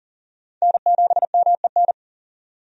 Morse code, heard as a keyed tone, about 40 wpm, 700 Hz. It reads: N7MEN
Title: USA Call Signs - 5 Characters - 40wpm